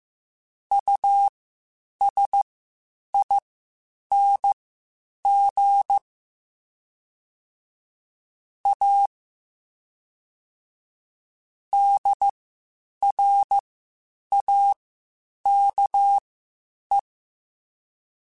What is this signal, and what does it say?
Morse code, heard as a keyed tone, about 15 wpm, spaced as for 8 wpm, 800 Hz. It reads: USING A DRAKE
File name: QSOs / QSOs-mp3s / WD0FBT 1@13